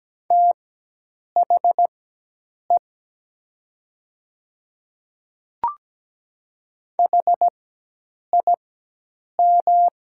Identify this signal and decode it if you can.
Morse code, heard as a keyed tone, about 17 wpm, spaced as for 8 wpm, 700 Hz. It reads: THE HIM
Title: 4x Character Spacing - Top 100 Words - 17wpm